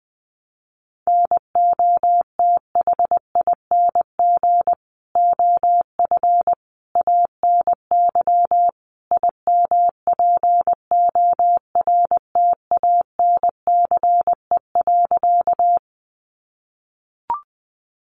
Morse code, heard as a keyed tone, about 20 wpm, 700 Hz. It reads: NOTHING OF ANY IMPORTANCE.
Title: Sentences from Top 1200 Words 20wpm